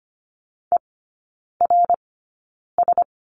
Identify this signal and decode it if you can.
Morse code, heard as a keyed tone, about 25 wpm, spaced as for 9 wpm, 700 Hz. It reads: ERS
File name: course / 25:10wpm / 093 Lesson 16e - Letter c - Three Characters 25wpm